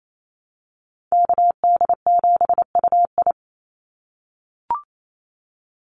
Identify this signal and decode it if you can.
Morse code, heard as a keyed tone, about 28 wpm, 700 Hz. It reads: KD7UI
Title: Call Signs - 28wpm